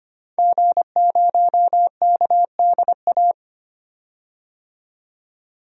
Morse code, heard as a keyed tone, about 25 wpm, 700 Hz. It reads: G0KDA
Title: DXCC-100 Call Signs - 5 Characters - 25wpm